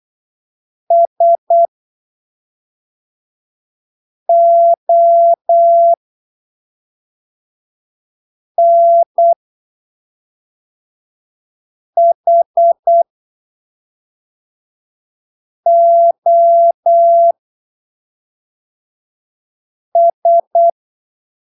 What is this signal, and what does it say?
Morse code, heard as a keyed tone, about 8 wpm, spaced as for 3 wpm, 680 Hz. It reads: SONHOS
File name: arquivos / texto / texto 15 08ppm